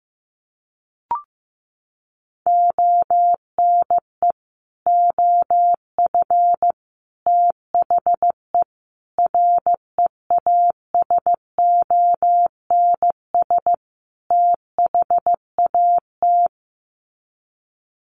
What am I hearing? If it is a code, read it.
Morse code, heard as a keyed tone, about 15 wpm, 700 Hz. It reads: ONE OF THE REASONS THAT
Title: Sets of 5 Words Encore 15wpm